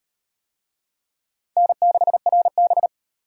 Morse code, heard as a keyed tone, about 34 wpm, 700 Hz. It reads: N6RB